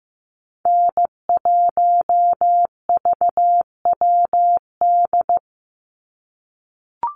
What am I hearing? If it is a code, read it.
Morse code, heard as a keyed tone, about 15 wpm, 700 Hz. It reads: N1VWD
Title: NA SOTA Chaser Callsigns 15wpm